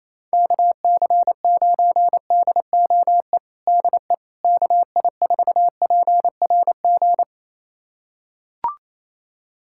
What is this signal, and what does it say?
Morse code, heard as a keyed tone, about 28 wpm, 700 Hz. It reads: KC9DOE DE KI4PRG
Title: USA Callsign DE Callsign - 6 Characters - 28wpm